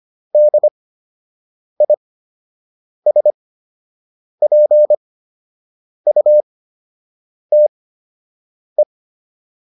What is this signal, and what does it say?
Morse code, heard as a keyed tone, about 25 wpm, spaced as for 7 wpm, 600 Hz. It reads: DISPUTE
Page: fundamental-practice-files